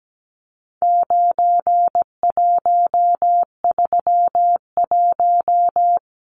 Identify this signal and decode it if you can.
Morse code, heard as a keyed tone, about 17 wpm, 700 Hz. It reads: 9131
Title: Numbers - 4-Digit 17wpm